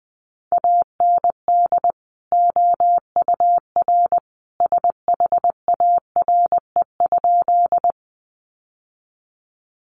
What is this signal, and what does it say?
Morse code, heard as a keyed tone, about 20 wpm, 700 Hz. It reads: AND OUR SHARE?
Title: Sentences from Top 900 Words 20wpm